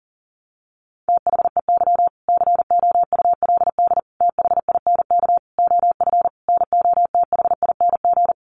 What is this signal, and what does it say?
Morse code, heard as a keyed tone, about 40 wpm, 700 Hz. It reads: THEY COULD THINK OF NOTHING
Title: Sentences from Top 800 Words 40wpm